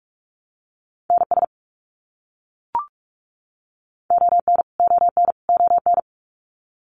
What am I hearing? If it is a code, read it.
Morse code, heard as a keyed tone, about 44 wpm, 700 Hz. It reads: NS ON ON ON
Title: Canadian Provinces and Territories - Repeated 3x - 45wpm